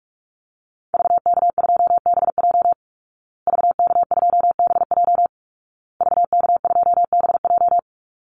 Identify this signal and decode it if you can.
Morse code, heard as a keyed tone, about 44 wpm, 700 Hz. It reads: VK2BJ VK2BJ VK2BJ
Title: Call Signs Repeated 2x - 45wpm